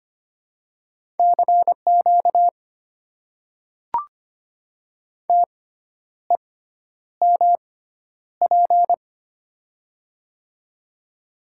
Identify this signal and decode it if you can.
Morse code, heard as a keyed tone, about 25 wpm, spaced as for 10 wpm, 700 Hz. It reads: CQ TEMP
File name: Top 90 QSO Elements - 6x Character Spacing - 25wpm